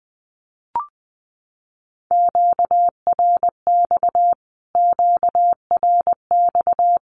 Morse code, heard as a keyed tone, about 19 wpm, 700 Hz. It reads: QRX QRX
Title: Q-Codes Repeated 3x 20wpm